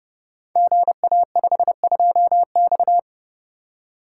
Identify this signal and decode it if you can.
Morse code, heard as a keyed tone, about 29 wpm, 700 Hz. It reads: GA52X
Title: Code Groups - 5 Letter:Numbers - 30wpm